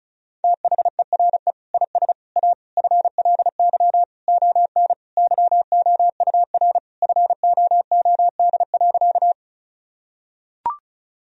Morse code, heard as a keyed tone, about 36 wpm, 700 Hz. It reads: THERE IS A FLY ON YOUR FOOD.
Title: Sentences from Top 1000 Words - Review Part 2 - 35wpm